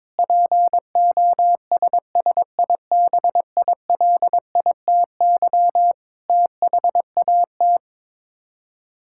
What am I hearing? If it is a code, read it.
Morse code, heard as a keyed tone, about 22 wpm, 700 Hz. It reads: POSSIBILITY THAT
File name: practice / 22wpm / Sets of 8 Words - 22wpm